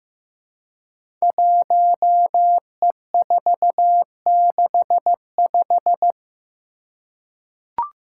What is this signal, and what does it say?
Morse code, heard as a keyed tone, about 15 wpm, 700 Hz. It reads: 1E465
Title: Code Groups - 5 Letter:Numbers - 15wpm